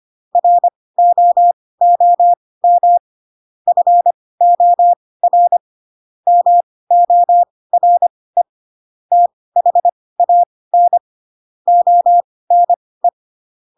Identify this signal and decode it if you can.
Morse code, heard as a keyed tone, about 25 wpm, spaced as for 18 wpm, 700 Hz. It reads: ROOM FOR MORE THAN ONE
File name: Pinocchio / Pinocchio ch4 12 25 ik1hge